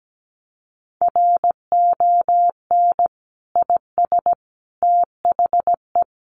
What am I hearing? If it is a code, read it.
Morse code, heard as a keyed tone, about 17 wpm, 700 Hz. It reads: RON IS THE
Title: Additional Easy Phrases 17wpm